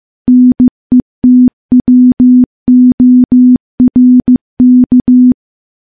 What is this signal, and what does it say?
Morse code, heard as a keyed tone, about 14 wpm, 250 Hz. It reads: NETWORK
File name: morse / morse-training-15WPM